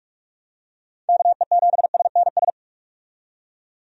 Morse code, heard as a keyed tone, about 46 wpm, 700 Hz. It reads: KE7SNS
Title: USA Call Signs - 6 Characters - 45wpm